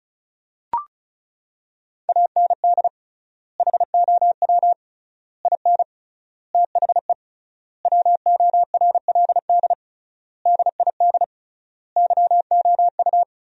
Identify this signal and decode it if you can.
Morse code, heard as a keyed tone, about 36 wpm, 700 Hz. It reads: AND HOW IN THE WORLD DID YOU
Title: 3x Word-Spacing - Sentences from Top 200 Words - 35wpm